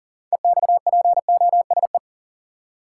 Morse code, heard as a keyed tone, about 39 wpm, 700 Hz. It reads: EXPOSE
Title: Six Letter Words - 40wpm